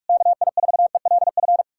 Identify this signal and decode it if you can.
Morse code, heard as a keyed tone, about 58 wpm, 700 Hz. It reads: KI4ELF